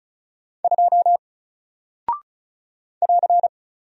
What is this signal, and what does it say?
Morse code, heard as a keyed tone, about 36 wpm, 700 Hz. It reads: 2 <AR>
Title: Single Character (Letter:Number:Punctuation:Prosign) Alt 35wpm